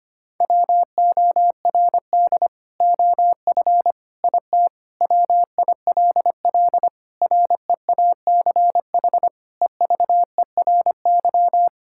Morse code, heard as a keyed tone, about 25 wpm, 700 Hz. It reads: WORD OF IT WILL REACH EVERY
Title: Sentences from Top 1600 Words 25wpm